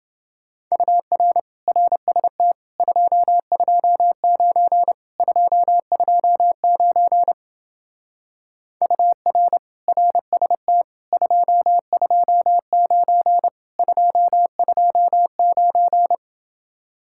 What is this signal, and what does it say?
Morse code, heard as a keyed tone, about 28 wpm, 700 Hz. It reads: UR RST 229 229 UR RST 229 229